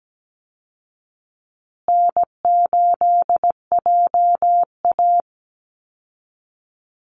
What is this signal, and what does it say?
Morse code, heard as a keyed tone, about 17 wpm, 700 Hz. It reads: N8JA
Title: Easy Call Signs Repeated 3x - 17wpm